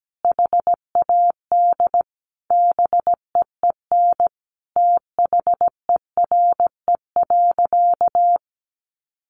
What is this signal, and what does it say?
Morse code, heard as a keyed tone, about 17 wpm, 700 Hz. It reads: HAD BEEN THERE.
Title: Sentences from Top 300 Words 17wpm